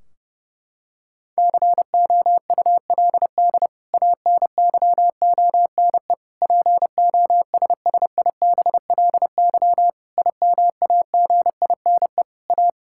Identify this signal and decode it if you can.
Morse code, heard as a keyed tone, about 30 wpm, 700 Hz. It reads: COULD ANYONE POSSIBLY IMAGINE A
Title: Sentences from Top 1600 Words 30wpm